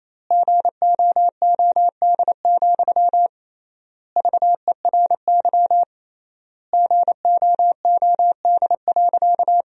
Morse code, heard as a keyed tone, about 28 wpm, 700 Hz. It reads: GOOD, VERY GOOD.
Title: 3x Word-Spacing - Sentences from Top 200 Words - 28wpm